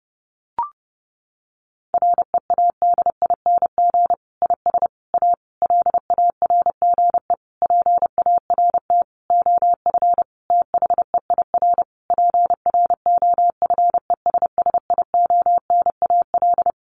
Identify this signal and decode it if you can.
Morse code, heard as a keyed tone, about 30 wpm, 700 Hz. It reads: READING IS A LARGE PART OF THEIR PROFESSIONAL
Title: Sentences from Top 1400 Words 30wpm